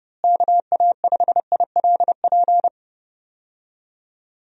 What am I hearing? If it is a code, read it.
Morse code, heard as a keyed tone, about 32 wpm, 700 Hz. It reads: KA5ILP
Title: USA Call Signs - 6 Characters - Repeated 2x - 30wpm